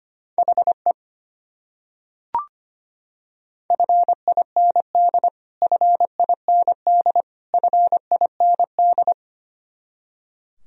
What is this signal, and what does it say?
Morse code, heard as a keyed tone, about 25 wpm, 700 Hz. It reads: HE FIND FIND FIND
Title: Top 100 Words Repeated 25wpm